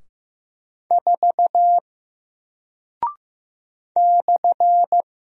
Morse code, heard as a keyed tone, about 15 wpm, 700 Hz. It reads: 4 /
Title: Single Character (Letter:Number:Punctuation:Prosign) Alt 15wpm